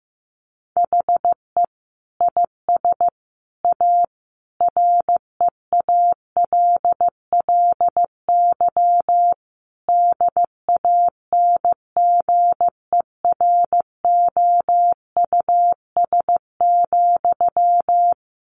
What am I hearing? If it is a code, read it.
Morse code, heard as a keyed tone, about 15 wpm, 700 Hz. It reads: HE IS A REALLY DANGEROUS,
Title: Sentences from Top 1200 Words 15wpm